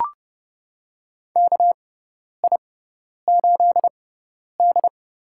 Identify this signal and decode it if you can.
Morse code, heard as a keyed tone, about 30 wpm, spaced as for 10 wpm, 700 Hz. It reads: KI8D
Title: USA Call Signs - 4 Characters - 6x Character Spacing - 30wpm